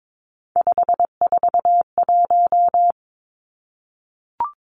Morse code, heard as a keyed tone, about 22 wpm, 700 Hz. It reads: 541
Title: Three Challenging Letter-Number Combinations 22wpm